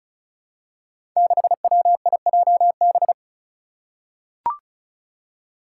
Morse code, heard as a keyed tone, about 36 wpm, 700 Hz. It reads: 6WIJB
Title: Five Challenging Letter-Number Combinations 35wpm